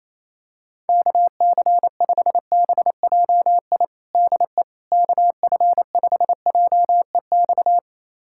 Morse code, heard as a keyed tone, about 27 wpm, 700 Hz. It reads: KC5BJI DE KF5JEX
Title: USA Callsign DE Callsign - 6 Characters - 28wpm